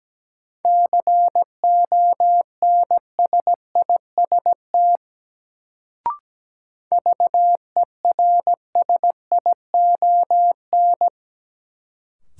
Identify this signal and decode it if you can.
Morse code, heard as a keyed tone, about 17 wpm, 700 Hz. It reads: CONSIST VERSION
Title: Sentences from Top 1400 Words - New Word Drill Alt - 17wpm